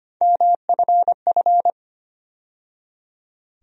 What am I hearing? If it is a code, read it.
Morse code, heard as a keyed tone, about 25 wpm, 700 Hz. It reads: MFF
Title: USA Call Sign Postfixes - 3 Letters 25wpm